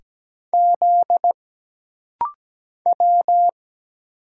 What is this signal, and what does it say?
Morse code, heard as a keyed tone, about 17 wpm, 700 Hz. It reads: Z W
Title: Single Letter-Number Alt - Rapid-Fire 17wpm